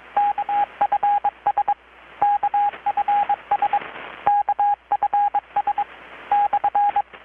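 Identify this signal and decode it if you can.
Morse code, heard as a keyed tone, about 22 wpm, 820 Hz, with radio noise. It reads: KFS KFS KFS /
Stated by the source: Morse from KPH's Night of Nights XX maritime radiotelegraphy event, received in Chiapas, Mexico about 3,800 km away on a Kenwood R-600 with a grounded 10 m long-wire antenna.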